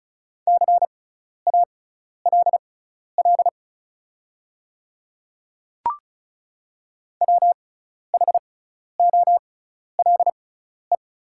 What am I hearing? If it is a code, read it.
Morse code, heard as a keyed tone, about 34 wpm, spaced as for 12 wpm, 690 Hz. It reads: CALL WHOLE
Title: 6x Character Spacing - Top 500 Words - 35wpm